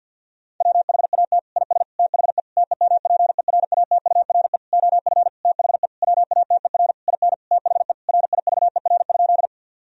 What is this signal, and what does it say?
Morse code, heard as a keyed tone, about 50 wpm, 700 Hz. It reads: WHAT IS THE TEMPERATURE OF THE WATER IN THE RIVER?
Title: Sentences from Top 1000 Words - Review Part 4 - 50wpm